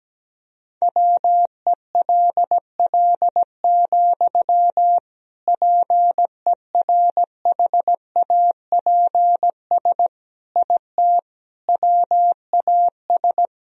Morse code, heard as a keyed tone, about 17 wpm, 700 Hz. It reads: WELL, PERHAPS IT WAS
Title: Sentences from Top 500 Words and Abbreviations 17wpm